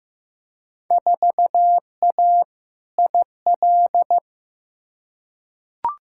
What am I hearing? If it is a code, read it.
Morse code, heard as a keyed tone, about 15 wpm, 700 Hz. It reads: 4A IL
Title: Field Day Exchanges - 15wpm